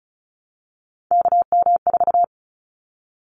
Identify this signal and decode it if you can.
Morse code, heard as a keyed tone, about 34 wpm, 700 Hz. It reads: KM4
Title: Callsign Prefixes for Top 25 DXCC - Repeated 3x - 35wpm